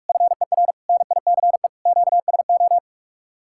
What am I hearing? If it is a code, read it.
Morse code, heard as a keyed tone, about 58 wpm, spaced as for 51 wpm, 700 Hz. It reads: FER NICE QSO
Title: Speed-Racing Common QSO Phrases - 45wpm